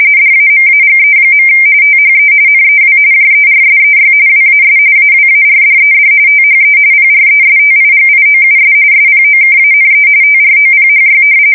RTTY, about 45.45 baud. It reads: DAMAGE. THE NEW JERSEY WAS AWARDED THE PRESTIGIOUS SPOKANE TROPHY
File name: fsk170 / battleshipnj